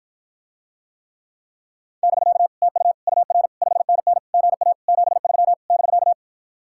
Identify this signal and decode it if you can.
Morse code, heard as a keyed tone, about 52 wpm, 700 Hz. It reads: <BK> TU UR 5NN GA 73 <BK>